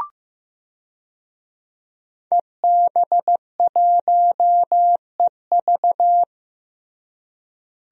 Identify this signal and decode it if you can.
Morse code, heard as a keyed tone, about 16 wpm, 700 Hz. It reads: EB1EV
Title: DXCC-100 Call Signs - 5 Characters - Repeated 2x - 15wpm